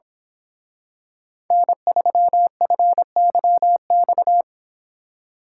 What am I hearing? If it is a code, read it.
Morse code, heard as a keyed tone, about 26 wpm, 700 Hz. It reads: N3FYX
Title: Speed Racing - USA 1x3 Call Signs 22wpm